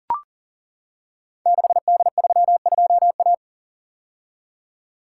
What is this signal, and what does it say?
Morse code, heard as a keyed tone, about 41 wpm, 690 Hz. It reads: 6D32A
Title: Code Groups - 5 Letter:Numbers - 40wpm